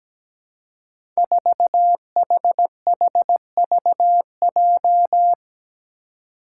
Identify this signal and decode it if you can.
Morse code, heard as a keyed tone, about 17 wpm, 700 Hz. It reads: 4HHVJ
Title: Five Challenging Letter-Number Combinations 17wpm